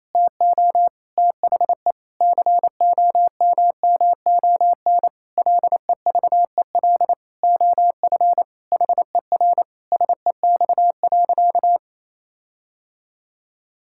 Morse code, heard as a keyed tone, about 28 wpm, 700 Hz. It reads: TO THE COMMON LEVEL OF HER SEX.
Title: Sentences from Top 1600 Words 28wpm